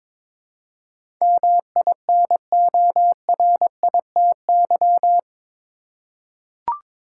Morse code, heard as a keyed tone, about 22 wpm, 700 Hz. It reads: MINORITY
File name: practice / 22wpm / Eight Letter Words - 22wpm